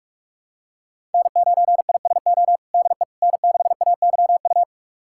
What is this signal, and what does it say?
Morse code, heard as a keyed tone, about 41 wpm, 700 Hz. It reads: N9ISO DE N6AYU
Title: USA Callsign DE Callsign - 5 Characters - 45wpm